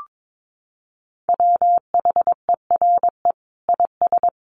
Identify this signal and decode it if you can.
Morse code, heard as a keyed tone, about 22 wpm, 700 Hz. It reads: WHERE IS
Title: Sentences from Top 1500 Words 22wpm